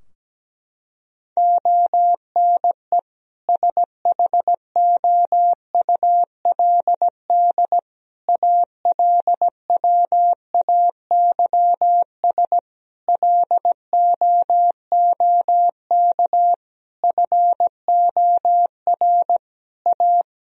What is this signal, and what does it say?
Morse code, heard as a keyed tone, about 17 wpm, 700 Hz. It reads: ONE SHOULD ALWAYS LOOK FOR A
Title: Sentences from Top 1000 Words - Review Part 4 - 17wpm